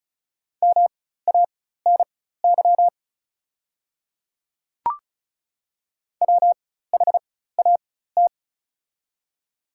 Morse code, heard as a keyed tone, about 36 wpm, spaced as for 16 wpm, 700 Hz. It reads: MANY WHAT
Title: 4x Character Spacing - Top 100 Words - 35wpm